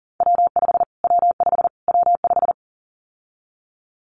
Morse code, heard as a keyed tone, about 40 wpm, 700 Hz. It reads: W5 W5 W5
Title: Callsign Prefixes for Top 25 DXCC - Repeated 3x - 40wpm